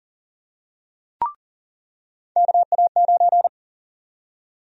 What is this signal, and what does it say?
Morse code, heard as a keyed tone, about 40 wpm, 700 Hz. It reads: KA9
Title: USA Call Sign Prefixes - 40wpm